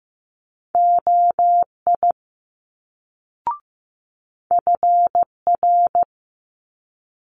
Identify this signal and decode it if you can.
Morse code, heard as a keyed tone, about 15 wpm, 700 Hz. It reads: OI FR
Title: Multi-letter Phonemes Quick Fire 15wpm